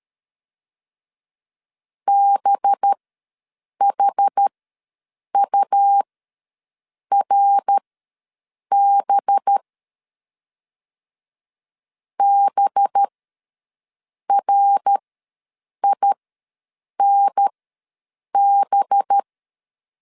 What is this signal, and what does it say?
Morse code, heard as a keyed tone, about 13 wpm, spaced as for 7 wpm, 800 Hz. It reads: BHURB BRINB
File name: 5 LB ws 12wpm